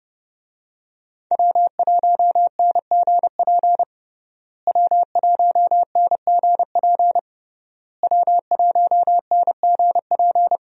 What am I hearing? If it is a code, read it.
Morse code, heard as a keyed tone, about 29 wpm, 700 Hz. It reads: W1NGP W1NGP W1NGP
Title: USA Call Signs - 5 Characters - Repeated 3x - 30wpm